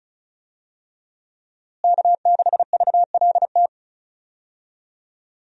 Morse code, heard as a keyed tone, about 36 wpm, 700 Hz. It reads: K6VLT
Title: USA Call Signs - 5 Characters - Repeated 3x - 35wpm